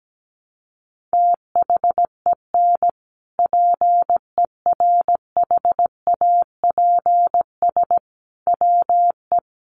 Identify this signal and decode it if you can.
Morse code, heard as a keyed tone, about 17 wpm, 700 Hz. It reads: THEN PERHAPS WE
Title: Sentences from Top 1100 Words 17wpm